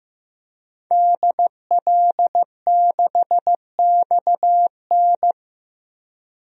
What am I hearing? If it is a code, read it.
Morse code, heard as a keyed tone, about 15 wpm, 700 Hz. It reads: DL6XN
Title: DXCC-100 Call Signs - 5 Characters - Repeated 2x - 15wpm